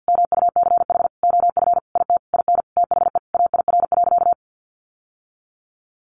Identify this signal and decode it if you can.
Morse code, heard as a keyed tone, about 50 wpm, 700 Hz. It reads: MUCH OF IT IN THE AIR.
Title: Sentences from Top 900 Words 50wpm